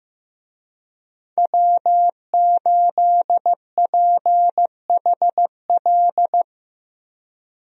Morse code, heard as a keyed tone, about 15 wpm, 700 Hz. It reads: W8PHL
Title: USA Call Signs - 5 Characters - 15wpm